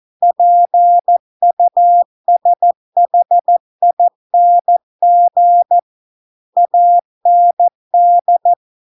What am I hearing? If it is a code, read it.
Morse code, heard as a keyed tone, about 14 wpm, 700 Hz. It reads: PUSHING AND